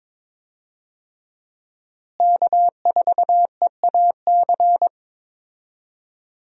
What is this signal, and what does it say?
Morse code, heard as a keyed tone, about 22 wpm, 700 Hz. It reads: K4EAC
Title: USA Call Signs - 5 Characters - Repeated 3x - 22wpm